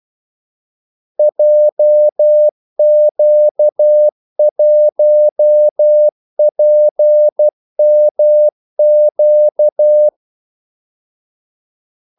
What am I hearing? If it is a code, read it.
Morse code, heard as a keyed tone, about 12 wpm, 600 Hz. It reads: JQ1PMQ
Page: Kaldesignaler 12 wpm